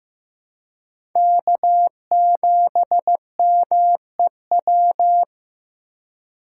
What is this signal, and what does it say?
Morse code, heard as a keyed tone, about 15 wpm, 700 Hz. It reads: K7MEW
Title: USA Call Signs - 5 Characters - 4x Character Spacing - 15wpm